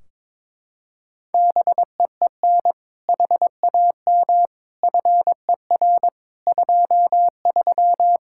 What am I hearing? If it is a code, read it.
Morse code, heard as a keyed tone, about 22 wpm, 700 Hz. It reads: BEEN HAM FER 23
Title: Practice QSO - Exchange 3 - More Info - 22wpm